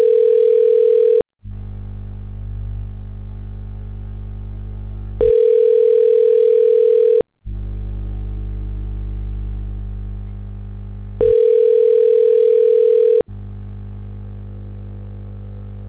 Ringback tone (North America), 440 + 480 Hz.
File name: recordings / RE a75a84cc39bb00e41f4c5d89cc